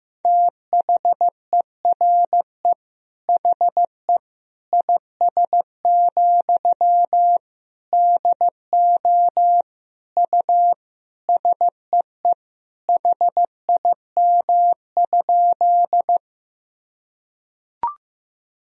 Morse code, heard as a keyed tone, about 15 wpm, 700 Hz. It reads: THERE HE IS, DO U SEE HIM?
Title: Sentences from Top 100 Words and Abbreviations 15wpm